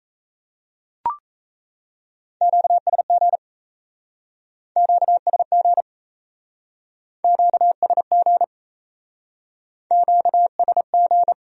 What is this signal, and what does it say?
Morse code, heard as a keyed tone, about 35 wpm, 700 Hz. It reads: QSG QSG QSG QSG
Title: Speed-Racing Q-Codes - 28wpm